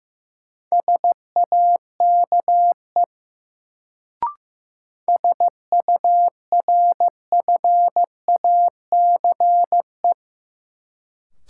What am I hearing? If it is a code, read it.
Morse code, heard as a keyed tone, about 15 wpm, 700 Hz. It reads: SAKE SURFACE
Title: Sets of 4 Words - New Word Drill - 15wpm